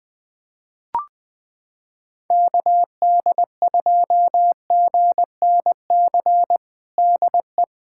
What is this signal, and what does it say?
Morse code, heard as a keyed tone, about 20 wpm, 700 Hz. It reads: KD2GNC DE
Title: USA Callsign DE Callsign - 6 Characters - 20wpm